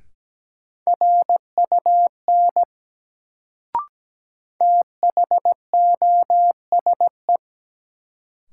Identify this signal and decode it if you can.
Morse code, heard as a keyed tone, about 17 wpm, 700 Hz. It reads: RUN THOSE
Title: Sentences from Top 300 Words - New Word Drill - 17wpm